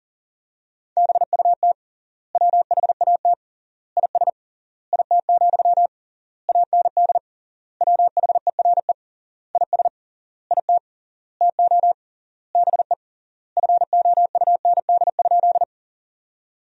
Morse code, heard as a keyed tone, about 40 wpm, 700 Hz. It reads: BUT WHAT IS IT, AND WHERE IS IT TO BE FOUND?
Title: 3x Word-Spacing - Sentences from Top 300 Words - 40wpm